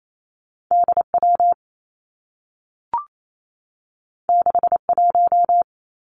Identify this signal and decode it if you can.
Morse code, heard as a keyed tone, about 28 wpm, 700 Hz. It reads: D W 6 1
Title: Two Challenging Letter-Number Combinations 28wpm